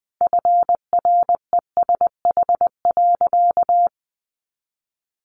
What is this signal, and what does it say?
Morse code, heard as a keyed tone, about 20 wpm, 700 Hz. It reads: FRESH.
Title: Sentences from Top 1500 Words 20wpm